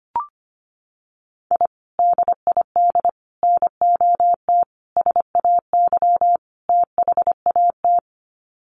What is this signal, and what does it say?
Morse code, heard as a keyed tone, about 25 wpm, 700 Hz. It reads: I DID NOT SAY THAT
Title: Sets of 5 Words Encore2 25wpm